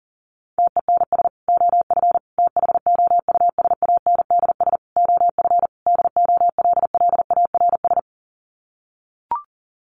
Morse code, heard as a keyed tone, about 40 wpm, 700 Hz. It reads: TENS OF THOUSANDS OF DOLLARS
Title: Sets of 5 Words Encore2 40wpm